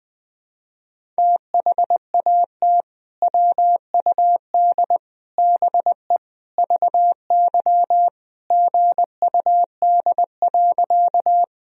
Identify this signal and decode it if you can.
Morse code, heard as a keyed tone, about 20 wpm, 700 Hz. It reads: THAT WUD BE VY GUD.
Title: Sentences from Top 200 Words and Abbreviations 20wpm